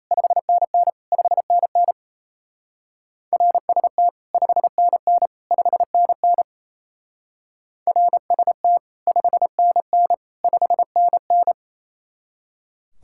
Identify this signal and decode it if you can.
Morse code, heard as a keyed tone, about 37 wpm, 700 Hz. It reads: 5NN 5NN RST 5NN 5NN RST 5NN 5NN